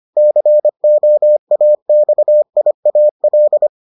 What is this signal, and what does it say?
Morse code, heard as a keyed tone, about 25 wpm, 600 Hz. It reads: COAXIAL